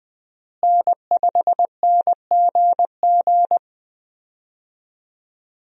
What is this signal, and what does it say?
Morse code, heard as a keyed tone, about 20 wpm, 700 Hz. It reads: N5NGG
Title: USA Call Signs - 5 Characters - Repeated 3x - 20wpm